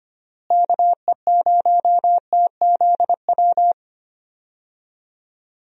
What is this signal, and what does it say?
Morse code, heard as a keyed tone, about 25 wpm, 700 Hz. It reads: KE0TZW
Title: USA Call Signs - 6 Characters - Repeated 3x - 25wpm